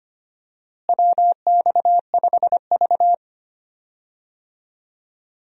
Morse code, heard as a keyed tone, about 25 wpm, 700 Hz. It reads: WX5V
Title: USA Call Signs - 4 Characters - Repeated 2x - 25wpm